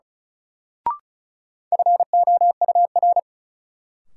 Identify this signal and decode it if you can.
Morse code, heard as a keyed tone, about 36 wpm, 700 Hz. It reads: FOUR
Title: Sentences from Top 300 Words - New Word Drill - 35wpm